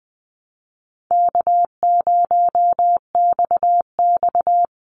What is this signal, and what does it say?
Morse code, heard as a keyed tone, about 20 wpm, 700 Hz. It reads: K0XX
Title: Easy Call Signs Repeated 3x - 20wpm